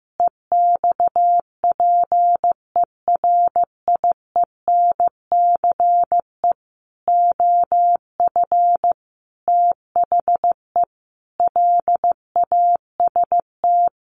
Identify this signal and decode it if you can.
Morse code, heard as a keyed tone, about 15 wpm, 700 Hz. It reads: EXPERIENCE OF THE LAST
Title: Sentences from Top 1500 Words 15wpm